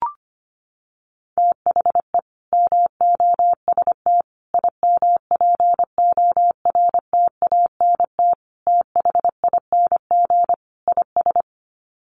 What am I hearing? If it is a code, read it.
Morse code, heard as a keyed tone, about 25 wpm, 700 Hz. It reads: THE MOST IMPORTANT THING IS
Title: Sets of 5 Words Encore 25wpm